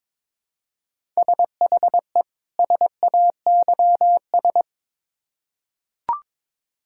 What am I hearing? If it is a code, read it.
Morse code, heard as a keyed tone, about 22 wpm, 700 Hz. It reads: SHE SAYS
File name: Build A Phrase - 2-Word Phrases - 22wpm